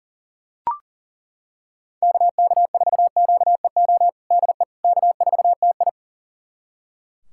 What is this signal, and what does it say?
Morse code, heard as a keyed tone, about 40 wpm, 700 Hz. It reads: KK4QEO DE K4TI
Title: USA Callsign DE Callsign 40wpm